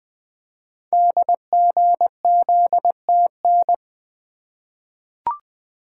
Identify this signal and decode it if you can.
Morse code, heard as a keyed tone, about 20 wpm, 700 Hz. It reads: DGZTN
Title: Code Groups - 5 Letter:Numbers - 20wpm